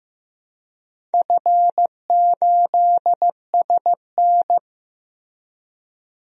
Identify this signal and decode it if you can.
Morse code, heard as a keyed tone, about 15 wpm, 700 Hz. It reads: F8SN
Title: DXCC-100 Call Signs - 4 Characters - 15wpm